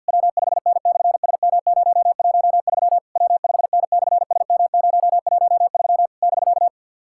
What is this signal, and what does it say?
Morse code, heard as a keyed tone, about 50 wpm, 700 Hz. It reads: W5N/SM013 W5N/SM013 <BK>
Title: SOTA QSO Practice 50wpm